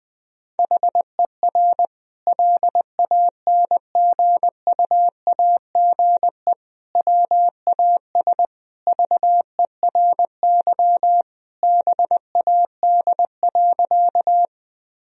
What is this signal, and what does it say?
Morse code, heard as a keyed tone, about 20 wpm, 700 Hz. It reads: HER LANGUAGE WAS VERY BAD.
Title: Sentences from Top 1000 Words - Review Part 3 - 20wpm